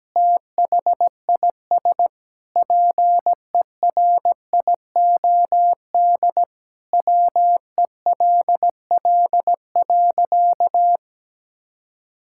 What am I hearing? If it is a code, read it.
Morse code, heard as a keyed tone, about 17 wpm, 700 Hz. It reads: THIS PERIOD WELL.
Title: Sentences from Top 1000 Words - Review Part 5 - 17wpm